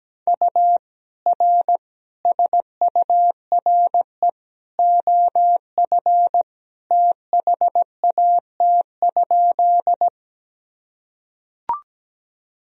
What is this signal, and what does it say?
Morse code, heard as a keyed tone, about 17 wpm, 700 Hz. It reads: U R SURE OF THAT?
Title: Sentences from Top 300 Words and Abbreviations 17wpm